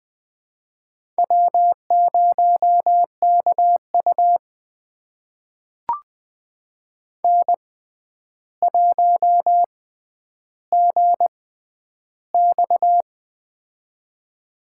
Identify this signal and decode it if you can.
Morse code, heard as a keyed tone, about 20 wpm, spaced as for 10 wpm, 700 Hz. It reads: W0KU N1GX
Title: USA Call Signs - 6x Character Spacing - 20wpm